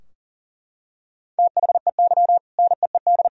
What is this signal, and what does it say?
Morse code, heard as a keyed tone, about 37 wpm, 700 Hz. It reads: THEY NEED
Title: Sentences from Top 1000 Words - Review Part 4 - 40wpm